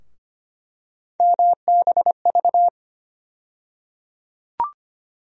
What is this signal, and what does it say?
Morse code, heard as a keyed tone, about 25 wpm, 700 Hz. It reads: MBV
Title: USA Call Sign Postfixes - 3 Letters 25wpm